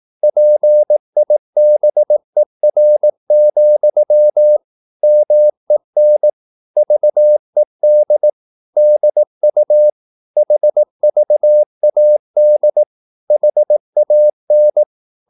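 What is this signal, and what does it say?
Morse code, heard as a keyed tone, about 18 wpm, 600 Hz. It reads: PIBER, MEN VED DU HVAD HAN
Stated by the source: Slow (QRS) Morse practice at 18 wpm, CW med Gnister.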